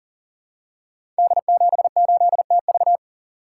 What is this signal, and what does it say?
Morse code, heard as a keyed tone, about 39 wpm, 690 Hz. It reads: D78TV
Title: Code Groups - 5 Letter:Numbers - 40wpm